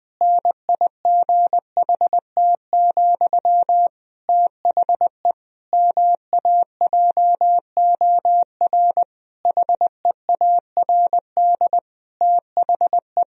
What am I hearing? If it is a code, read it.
Morse code, heard as a keyed tone, about 20 wpm, 700 Hz. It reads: NIGHT, THE MAJOR HEARD THE
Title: Sentences from Top 1000 Words - Review Part 4 - 20wpm